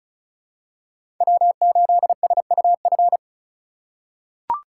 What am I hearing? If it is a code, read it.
Morse code, heard as a keyed tone, about 36 wpm, 700 Hz. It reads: W8SUF
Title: USA Call Signs - 5 Characters - 3x Character Spacing - 35wpm